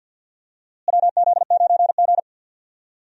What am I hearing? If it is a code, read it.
Morse code, heard as a keyed tone, about 51 wpm, 700 Hz. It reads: WZ9G
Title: USA Call Signs - 4 Characters - Repeated 3x - 50wpm